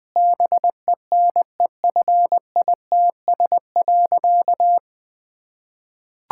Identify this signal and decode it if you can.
Morse code, heard as a keyed tone, about 21 wpm, 700 Hz. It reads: BENEFITS.
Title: Difficult Sentences with Hints - 20wpm